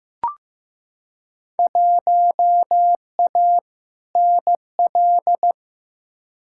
Morse code, heard as a keyed tone, about 15 wpm, 700 Hz. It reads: 1A NL
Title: Field Day Exchanges Encore - 15wpm